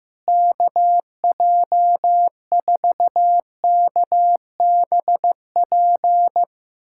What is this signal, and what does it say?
Morse code, heard as a keyed tone, about 15 wpm, 700 Hz. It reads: KJ4KBP
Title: USA Callsign DE Callsign 15wpm